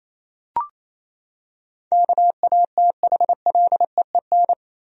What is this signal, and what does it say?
Morse code, heard as a keyed tone, about 28 wpm, 700 Hz. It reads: KATHLEEN
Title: USA Common Names - 28wpm